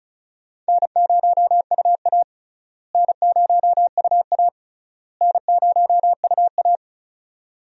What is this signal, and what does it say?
Morse code, heard as a keyed tone, about 36 wpm, 700 Hz. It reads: N0UA N0UA N0UA
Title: USA Call Signs - 4 Characters - Repeated 3x - 35wpm